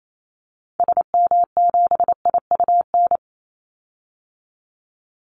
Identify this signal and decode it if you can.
Morse code, heard as a keyed tone, about 28 wpm, 700 Hz. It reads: SM7IUN
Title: Call Signs Repeated 3x - 28wpm